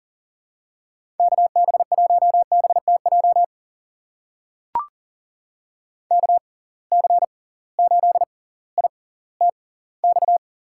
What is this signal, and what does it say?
Morse code, heard as a keyed tone, about 40 wpm, spaced as for 20 wpm, 700 Hz. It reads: KB1BTJ KC8ITX
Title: USA Call Signs - 6 Characters - 6x Character Spacing - 40wpm